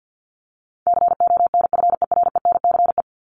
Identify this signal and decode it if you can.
Morse code, heard as a keyed tone, about 50 wpm, 700 Hz. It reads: CONFERENCE
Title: Top 1751-2000 Words 50wpm